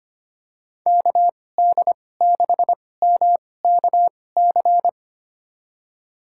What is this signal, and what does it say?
Morse code, heard as a keyed tone, about 25 wpm, spaced as for 18 wpm, 700 Hz. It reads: KD6MKC
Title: USA Call Signs - 6 Characters - 2x Character Spacing - 25wpm